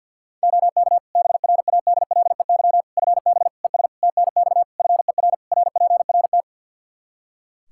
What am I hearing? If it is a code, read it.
Morse code, heard as a keyed tone, about 52 wpm, 700 Hz. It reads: OK BRADLEY FB ES TNX FER RPRT
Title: Practice QSO - Exchange 2 - Info - 50wpm